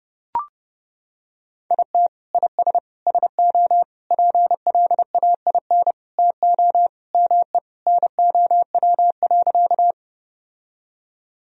Morse code, heard as a keyed tone, about 30 wpm, 700 Hz. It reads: IT IS SO PLAIN TO ME NOW.
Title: Sentences from Top 1000 Words - Review Part 2 - 30wpm